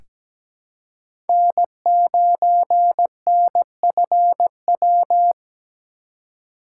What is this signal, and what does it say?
Morse code, heard as a keyed tone, about 17 wpm, 700 Hz. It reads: N9NFW
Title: USA Call Signs - 4x Character Spacing - 17wpm